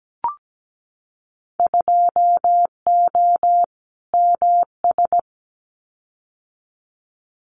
Morse code, heard as a keyed tone, about 17 wpm, 700 Hz. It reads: 2O MS
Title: Winter Field Day Exchanges 17wpm